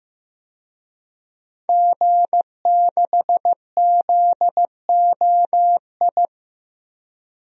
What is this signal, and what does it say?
Morse code, heard as a keyed tone, about 15 wpm, 700 Hz. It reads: G6ZOI
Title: DXCC-100 Call Signs - 5 Characters - 15wpm